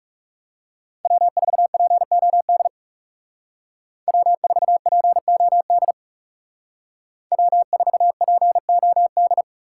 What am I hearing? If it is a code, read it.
Morse code, heard as a keyed tone, about 35 wpm, 700 Hz. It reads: W4POD W4POD W4POD
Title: Speed Racing - USA 1x3 Call Signs 30wpm